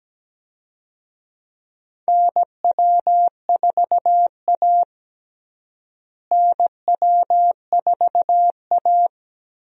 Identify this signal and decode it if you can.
Morse code, heard as a keyed tone, about 17 wpm, 700 Hz. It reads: NW4A NW4A
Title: USA Call Signs - 4 Characters - Repeated 3x - 17wpm